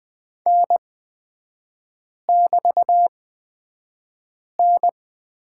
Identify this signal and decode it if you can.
Morse code, heard as a keyed tone, about 20 wpm, spaced as for 5 wpm, 700 Hz. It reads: N<BT>N
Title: 255 Lesson 41e - Period - Three Characters 20wpm